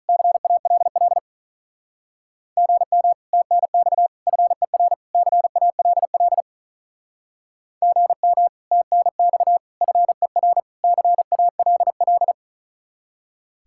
Morse code, heard as a keyed tone, about 46 wpm, 700 Hz. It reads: CALL GM TNX FER CALL GM TNX FER CALL